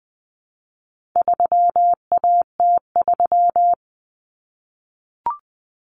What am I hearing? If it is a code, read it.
Morse code, heard as a keyed tone, about 20 wpm, 700 Hz. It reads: 3AT3
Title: Common Cut Numbers - 4-Digit 20wpm